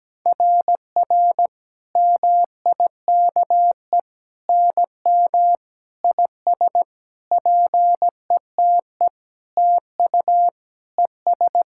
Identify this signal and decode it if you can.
Morse code, heard as a keyed tone, about 17 wpm, 700 Hz. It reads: RR MIKE NM IS PETE TU ES
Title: SOTA QSO Practice 17wpm